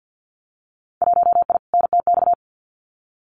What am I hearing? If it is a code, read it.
Morse code, heard as a keyed tone, about 50 wpm, 700 Hz. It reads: 1I NTX
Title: Winter Field Day Exchanges Encore 50wpm